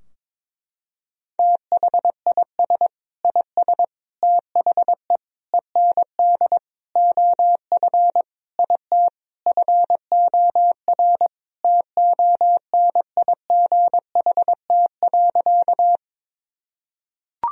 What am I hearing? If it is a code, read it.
Morse code, heard as a keyed tone, about 21 wpm, 700 Hz. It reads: THIS IS THE END OF IT FOR TONIGHT.
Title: Sentences from Top 1000 Words - Review Part 3 - 22wpm